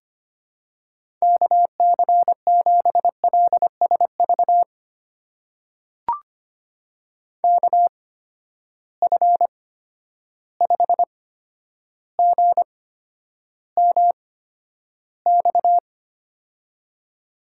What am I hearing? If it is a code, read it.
Morse code, heard as a keyed tone, about 25 wpm, spaced as for 11 wpm, 700 Hz. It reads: KC7LSV KF5GMX
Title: USA Call Signs - 6 Characters - 8x Character Spacing - 25wpm